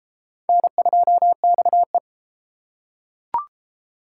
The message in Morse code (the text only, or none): N2XE